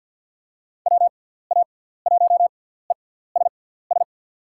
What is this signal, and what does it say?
Morse code, heard as a keyed tone, about 47 wpm, spaced as for 17 wpm, 700 Hz. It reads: WA1ESS
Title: USA Call Signs - 6x Character Spacing - 50wpm